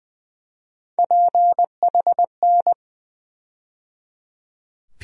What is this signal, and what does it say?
Morse code, heard as a keyed tone, about 20 wpm, 700 Hz. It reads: PHN
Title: USA Call Sign Postfixes - 3 Letters 20wpm